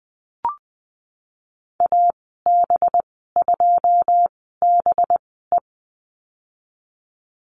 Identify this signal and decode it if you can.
Morse code, heard as a keyed tone, about 20 wpm, spaced as for 15 wpm, 700 Hz. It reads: AB2BE
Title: 231 Lesson 37g - Letter x - Call Signs 20wpm